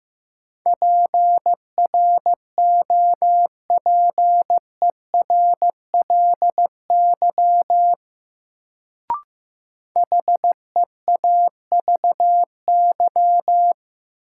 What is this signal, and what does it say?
Morse code, heard as a keyed tone, about 15 wpm, 700 Hz. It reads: PROPERLY HEAVY
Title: Sentences from Top 1100 Words - New Word Drill Alt - 15wpm